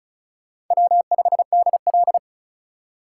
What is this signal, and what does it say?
Morse code, heard as a keyed tone, about 45 wpm, 700 Hz. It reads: W5DL